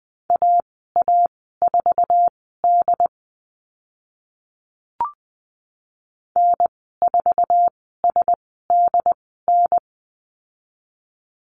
Morse code, heard as a keyed tone, about 20 wpm, spaced as for 14 wpm, 700 Hz. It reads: AA4D N4SDN
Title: 069 Lesson 12g - Letter d - Call Signs 20wpm